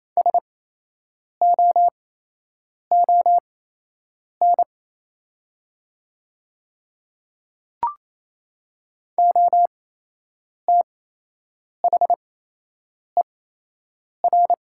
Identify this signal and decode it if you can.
Morse code, heard as a keyed tone, about 28 wpm, spaced as for 8 wpm, 700 Hz. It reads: SOON OTHER
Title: 8x Character Spacing - Top 500 Words - 28wpm